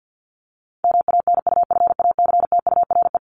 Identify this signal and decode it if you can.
Morse code, heard as a keyed tone, about 50 wpm, 700 Hz. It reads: MANUFACTURE
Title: ING Suffix - Top 100 Words with Base Words 50wpm